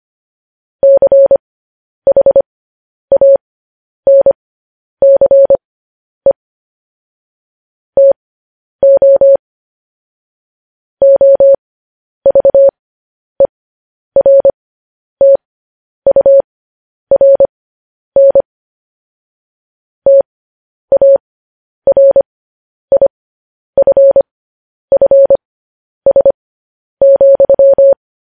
Morse code, heard as a keyed tone, about 25 wpm, spaced as for 10 wpm, 550 Hz. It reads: CHANCE TO OVERTURN TARIFFS,
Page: News Headlines in Morse Code at 10 WPM